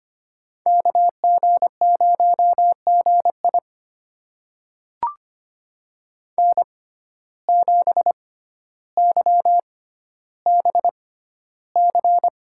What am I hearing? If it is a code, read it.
Morse code, heard as a keyed tone, about 25 wpm, spaced as for 13 wpm, 700 Hz. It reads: KG0GI N7YBC
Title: USA Call Signs - 5 Characters - 6x Character Spacing - 25wpm